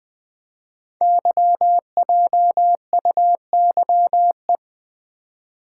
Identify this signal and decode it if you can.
Morse code, heard as a keyed tone, about 20 wpm, 700 Hz. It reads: YJUYE
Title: Code Groups - 5 Letters - 20wpm